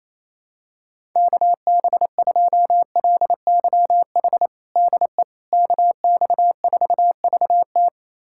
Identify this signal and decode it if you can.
Morse code, heard as a keyed tone, about 28 wpm, 700 Hz. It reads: KB2LYH DE KX4VT
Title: USA Callsign DE Callsign 28wpm